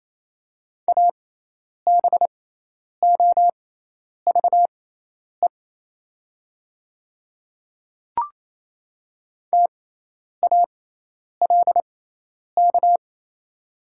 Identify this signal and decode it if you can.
Morse code, heard as a keyed tone, about 27 wpm, spaced as for 10 wpm, 700 Hz. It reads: ABOVE TALK
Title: 6x Character Spacing - Top 500 Words - 28wpm